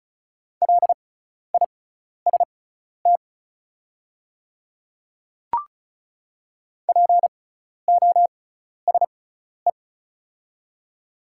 Morse code, heard as a keyed tone, about 36 wpm, spaced as for 12 wpm, 690 Hz. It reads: LIST POSE
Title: 6x Character Spacing - Top 500 Words - 35wpm